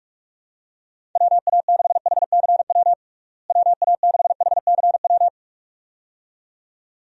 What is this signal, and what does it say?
Morse code, heard as a keyed tone, about 42 wpm, 700 Hz. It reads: WA6HCW WA6HCW
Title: USA Call Signs - 6 Characters - Repeated 2x - 45wpm